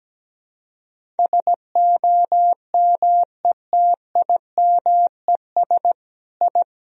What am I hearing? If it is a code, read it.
Morse code, heard as a keyed tone, about 17 wpm, 700 Hz. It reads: SOMETIMES I
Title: Sentences from Top 1000 Words - Review Part 3 - 17wpm